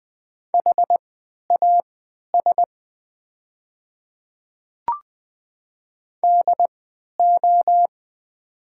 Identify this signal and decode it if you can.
Morse code, heard as a keyed tone, about 20 wpm, spaced as for 11 wpm, 700 Hz. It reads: HAS DO
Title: 3x Character Spacing - Top 200 Words - 20wpm